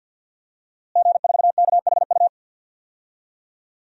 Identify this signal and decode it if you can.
Morse code, heard as a keyed tone, about 52 wpm, 700 Hz. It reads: G4KHU
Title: DXCC-100 Call Signs - 5 Characters - Repeated 3x - 50wpm